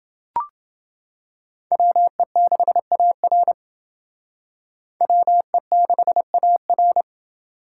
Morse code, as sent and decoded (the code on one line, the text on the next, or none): .-- . -.... .- .-. / .-- . -.... .- .-.
WE6AR WE6AR